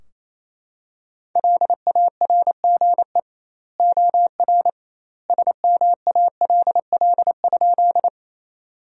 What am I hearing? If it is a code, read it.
Morse code, heard as a keyed tone, about 28 wpm, 700 Hz. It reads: LARGE OR SMALL?
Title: 2x Word-Spacing - Sentences from Top 200 Words - 28wpm